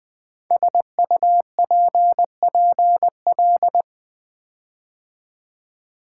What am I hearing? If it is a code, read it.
Morse code, heard as a keyed tone, about 20 wpm, 700 Hz. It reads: SUPPL
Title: Build A Word - 5 Letters At A Time - 10-Letter Words - 20wpm